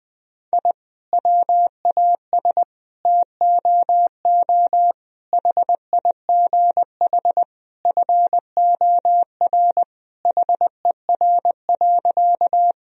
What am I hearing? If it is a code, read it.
Morse code, heard as a keyed tone, about 20 wpm, 700 Hz. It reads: I WAS TOO HIGH FOR HER.
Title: Sentences from Top 1000 Words - Review Part 1 - 20wpm